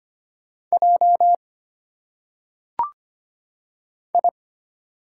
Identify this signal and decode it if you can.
Morse code, heard as a keyed tone, about 25 wpm, 700 Hz. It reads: J I
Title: Single Letters - Phonetic Pronunciation 25wpm